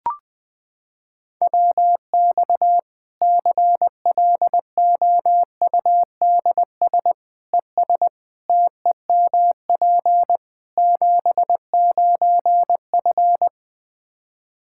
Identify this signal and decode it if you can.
Morse code, heard as a keyed tone, about 20 wpm, 700 Hz. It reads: WX CLOUDS ES TEMP 79F
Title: Practice QSO - Exchange 2 - Info - 20wpm